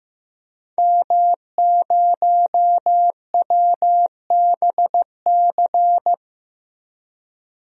Morse code, heard as a keyed tone, about 15 wpm, 700 Hz. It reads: M0WBC
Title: DXCC-100 Call Signs - 5 Characters - 15wpm